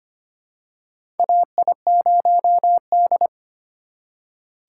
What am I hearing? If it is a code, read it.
Morse code, heard as a keyed tone, about 25 wpm, 700 Hz. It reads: AI0D
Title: USA Call Signs - 4 Characters - 2x Character Spacing - 25wpm